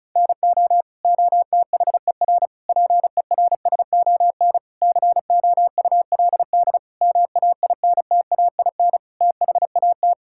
Morse code, heard as a keyed tone, about 36 wpm, 700 Hz. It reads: NO OTHER PERSON COULD MAINTAIN THAT
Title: Sentences from Top 1000 Words - Review Part 5 - 35wpm